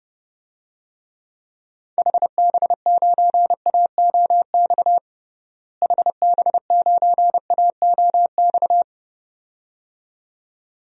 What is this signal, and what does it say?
Morse code, heard as a keyed tone, about 32 wpm, 700 Hz. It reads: HB9AOX HB9AOX
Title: DXCC-100 Call Signs - 6 Characters - Repeated 2x - 30wpm